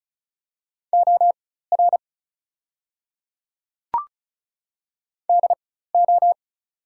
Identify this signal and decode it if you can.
Morse code, heard as a keyed tone, about 36 wpm, spaced as for 16 wpm, 700 Hz. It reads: OR DO
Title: 4x Character Spacing - Top 200 Words - 35wpm